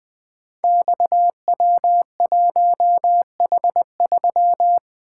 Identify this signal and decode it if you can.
Morse code, heard as a keyed tone, about 20 wpm, 700 Hz. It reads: XW1H3
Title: Code Groups - 5 Letter:Numbers - 20wpm